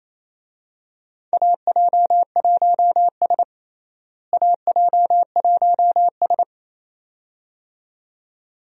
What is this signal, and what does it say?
Morse code, heard as a keyed tone, about 28 wpm, 700 Hz. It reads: AJ1S AJ1S
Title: USA Call Signs - 4 Characters - Repeated 2x - 28wpm